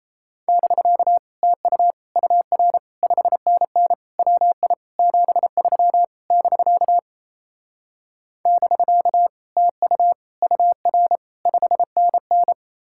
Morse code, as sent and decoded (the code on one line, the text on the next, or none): -...-.- / - ..- / ..- .-. / ..... -. -. / .-- .. / --... ...-- / -...-.- / -...-.- / - ..- / ..- .-. / ..... -. -.
<BK> TU UR 5NN WI 73 <BK> <BK> TU UR 5NN